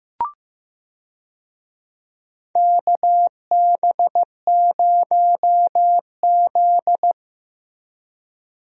Morse code, heard as a keyed tone, about 15 wpm, 700 Hz. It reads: KB0Z
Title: USA Call Signs - 4 Characters - Repeated 2x - 15wpm